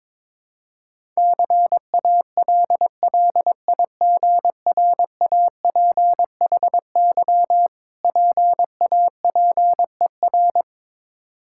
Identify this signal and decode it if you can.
Morse code, heard as a keyed tone, about 22 wpm, 690 Hz. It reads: CALLIGRAPHY PAPER
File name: Difficult Words with Hints - 22wpm